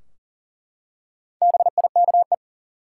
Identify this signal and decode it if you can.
Morse code, heard as a keyed tone, about 39 wpm, 700 Hz. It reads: BIKE
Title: Four Letter Words - 40wpm